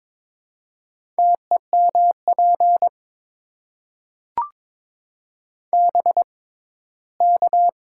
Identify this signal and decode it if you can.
Morse code, heard as a keyed tone, about 22 wpm, spaced as for 15 wpm, 700 Hz. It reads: TEMP BK
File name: Top 90 QSO Elements - 6x Character Spacing - 22wpm